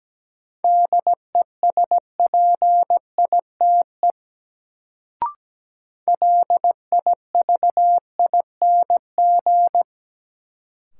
Morse code, heard as a keyed tone, about 17 wpm, 700 Hz. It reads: DESPITE LIVING
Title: Sets of 4 Words - New Word Drill - 17wpm